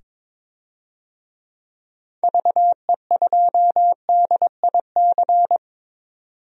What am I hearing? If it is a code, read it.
Morse code, heard as a keyed tone, about 23 wpm, 700 Hz. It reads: VE2DIC
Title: DXCC-100 Call Signs - 6 Characters - 22wpm